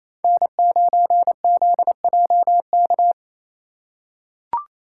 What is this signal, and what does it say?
Morse code, heard as a keyed tone, about 28 wpm, 700 Hz. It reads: N9ZJK
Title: USA Call Signs - 4x Character Spacing - 28wpm